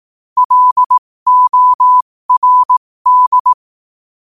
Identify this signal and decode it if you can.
Morse code, heard as a keyed tone, about 18 wpm, spaced as for 15 wpm, 1000 Hz. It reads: LORD
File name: qotdfiles / 15wpm